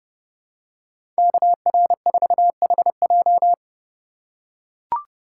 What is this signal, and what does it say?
Morse code, heard as a keyed tone, about 30 wpm, 700 Hz. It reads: KR4HJ
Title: USA Call Signs - 5 Characters - 3x Character Spacing - 30wpm